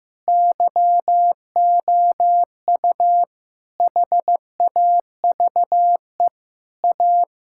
Morse code, heard as a keyed tone, about 15 wpm, 700 Hz. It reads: YOU HAVE A
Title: Sentences from Top 1000 Words - Review Part 5 - 15wpm